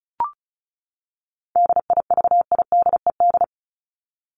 Morse code, heard as a keyed tone, about 34 wpm, 700 Hz. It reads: DIVIDED
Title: ED Suffix - Top 100 Words with Base Words 35wpm